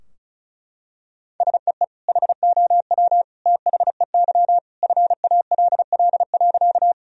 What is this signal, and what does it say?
Morse code, heard as a keyed tone, about 36 wpm, 700 Hz. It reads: SEE HOW THEY FALL.
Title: Sentences from Top 1000 Words - Review Part 2 - 35wpm